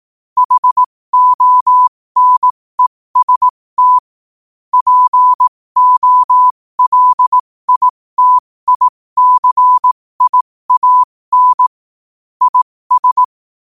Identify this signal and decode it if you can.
Morse code, heard as a keyed tone, about 18 wpm, spaced as for 15 wpm, 1000 Hz. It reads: HONEST POLITICIAN IS
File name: qotdfiles / 15wpm